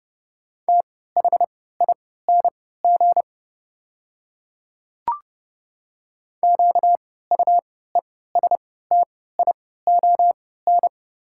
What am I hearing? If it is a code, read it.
Morse code, heard as a keyed tone, about 31 wpm, spaced as for 17 wpm, 700 Hz. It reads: THING QUESTION
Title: 3x Character Spacing - Top 500 Words - 30wpm